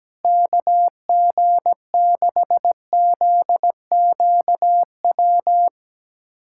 Morse code, heard as a keyed tone, about 17 wpm, 700 Hz. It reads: KG6ZQW
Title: USA Callsign DE Callsign - 6 Characters - 17wpm